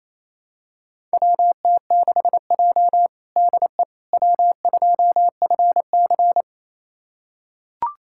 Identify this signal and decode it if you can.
Morse code, heard as a keyed tone, about 27 wpm, 700 Hz. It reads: WT6J DE W2FC
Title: USA Callsign DE Callsign - 4 Characters - 28wpm